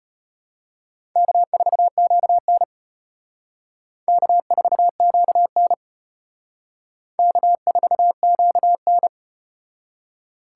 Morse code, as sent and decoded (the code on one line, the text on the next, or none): -.- ....- --.- -. / -.- ....- --.- -. / -.- ....- --.- -.
K4QN K4QN K4QN